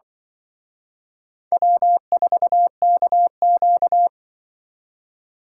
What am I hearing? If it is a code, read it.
Morse code, heard as a keyed tone, about 24 wpm, 700 Hz. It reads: W4KQ
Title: Speed Racing - USA 1x2 Call Signs 20wpm